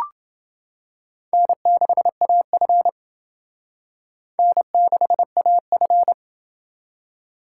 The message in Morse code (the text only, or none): N6AF N6AF